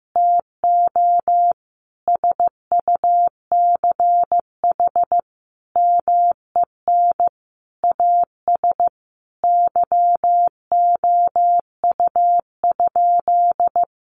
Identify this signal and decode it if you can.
Morse code, heard as a keyed tone, about 14 wpm, 700 Hz. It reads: TO SUCH MEN AS YOU?
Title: Sentences from Top 1000 Words 15wpm